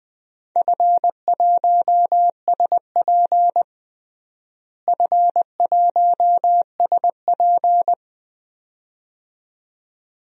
Morse code, heard as a keyed tone, about 21 wpm, 700 Hz. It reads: F1SP F1SP
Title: DXCC-100 Call Signs - 4 Characters - Repeated 2x - 20wpm